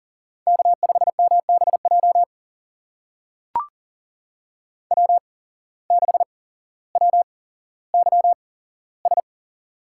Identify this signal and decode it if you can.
Morse code, heard as a keyed tone, about 39 wpm, spaced as for 17 wpm, 700 Hz. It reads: K5MBJ W6WYS
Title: USA Call Signs - 5 Characters - 8x Character Spacing - 40wpm